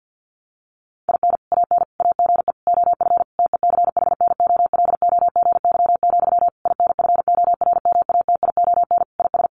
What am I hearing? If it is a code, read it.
Morse code, heard as a keyed tone, about 50 wpm, 700 Hz. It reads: IN AN AGE OF TECHNOLOGY, INFORMATION IS
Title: Sentences from Top 1200 Words 50wpm